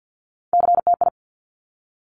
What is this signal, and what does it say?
Morse code, heard as a keyed tone, about 50 wpm, 700 Hz. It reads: CTI
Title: Top Three Letter Combinations in Words 50wpm